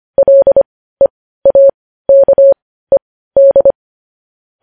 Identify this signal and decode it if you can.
Morse code, heard as a keyed tone, about 25 wpm, spaced as for 15 wpm, 550 Hz. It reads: LEAKED
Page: News Headlines in Morse Code at 15 WPM